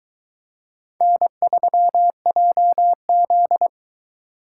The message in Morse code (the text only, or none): N3JZ